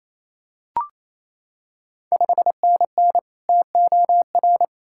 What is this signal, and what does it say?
Morse code, heard as a keyed tone, about 29 wpm, 690 Hz. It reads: 5NN TOR
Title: Ontario QSO Party Exchange 28wpm